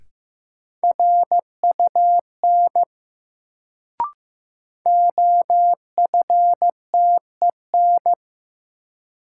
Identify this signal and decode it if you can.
Morse code, heard as a keyed tone, about 15 wpm, 700 Hz. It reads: RUN OFTEN
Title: Sentences from Top 300 Words - New Word Drill - 15wpm